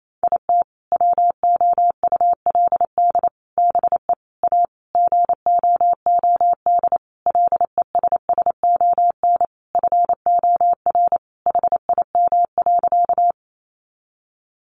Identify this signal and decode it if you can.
Morse code, heard as a keyed tone, about 28 wpm, 700 Hz. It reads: IT WOULD BE A GOOD LESSON FOR HIM.
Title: Sentences from Top 1200 Words 28wpm